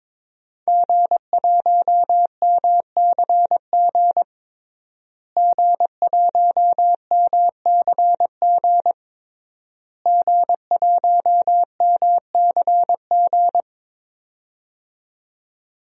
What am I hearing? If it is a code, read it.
Morse code, heard as a keyed tone, about 22 wpm, 700 Hz. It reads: G1MCG G1MCG G1MCG
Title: DXCC-100 Call Signs - 5 Characters - Repeated 3x - 22wpm